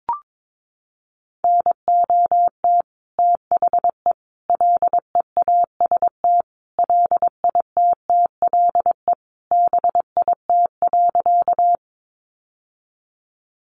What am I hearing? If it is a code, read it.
Morse code, heard as a keyed tone, about 22 wpm, 700 Hz. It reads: NOT THE LEAST LITTLE BIT.
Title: Sentences from Top 900 Words 22wpm